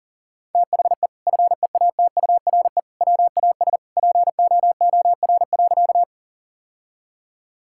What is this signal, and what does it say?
Morse code, heard as a keyed tone, about 40 wpm, 700 Hz. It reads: THE FEATURE WAS POOR.
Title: Sentences from Top 1000 Words - Review Part 5 - 40wpm